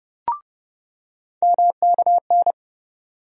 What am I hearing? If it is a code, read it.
Morse code, heard as a keyed tone, about 32 wpm, 700 Hz. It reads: MKN
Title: USA Call Sign Postfixes - 3 Letters 30wpm